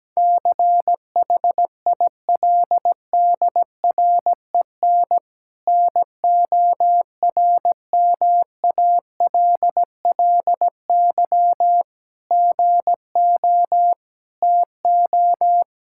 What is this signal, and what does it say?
Morse code, heard as a keyed tone, about 17 wpm, 700 Hz. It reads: CHILDREN NORMALLY GO TO
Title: Sentences from Top 1000 Words - Review Part 6 - 17wpm